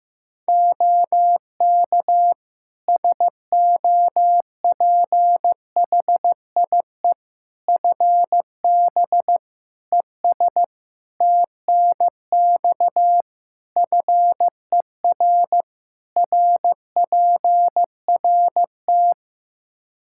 Morse code, heard as a keyed tone, about 15 wpm, 700 Hz. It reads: OK SOPHIE FB ES TNX FER RPRT
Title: Practice QSO - Exchange 2 - Info - 15wpm